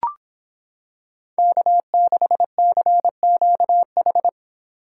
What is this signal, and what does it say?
Morse code, heard as a keyed tone, about 26 wpm, 700 Hz. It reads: K6CQH